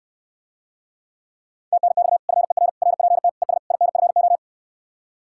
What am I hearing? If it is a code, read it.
Morse code, heard as a keyed tone, about 69 wpm, 700 Hz. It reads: TNX FER RPT ES INFO